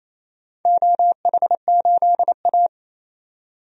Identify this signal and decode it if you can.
Morse code, heard as a keyed tone, about 27 wpm, 700 Hz. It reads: OH8A
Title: DXCC-100 Call Signs - 4 Characters - 28wpm